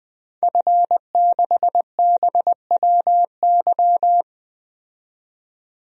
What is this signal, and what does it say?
Morse code, heard as a keyed tone, about 20 wpm, 700 Hz. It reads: F6BWY
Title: DXCC-100 Call Signs - 5 Characters - 20wpm